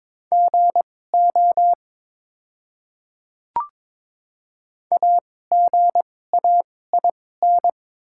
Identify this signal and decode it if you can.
Morse code, heard as a keyed tone, about 21 wpm, spaced as for 16 wpm, 700 Hz. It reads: GO AGAIN
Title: 2x Character Spacing - Top 200 Words - 22wpm